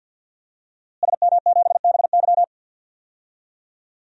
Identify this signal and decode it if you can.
Morse code, heard as a keyed tone, about 47 wpm, 700 Hz. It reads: SM7BY
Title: DXCC-100 Call Signs - 5 Characters - Repeated 2x - 50wpm